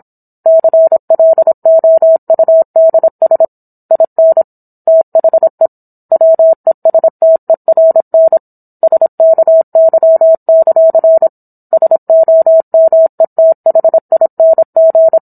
Morse code, heard as a keyed tone, about 27 wpm, 650 Hz. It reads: CLOUDS IN THE WESTERN SKY; SOMETHING